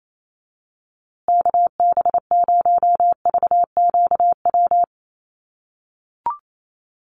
Morse code, heard as a keyed tone, about 28 wpm, 700 Hz. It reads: KB0VQW
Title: USA Call Signs Repeated 2x 28wpm